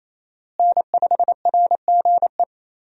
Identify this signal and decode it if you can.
Morse code, heard as a keyed tone, about 28 wpm, 700 Hz. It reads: N5RGE
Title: USA Call Signs - 5 Characters - Repeated 2x - 28wpm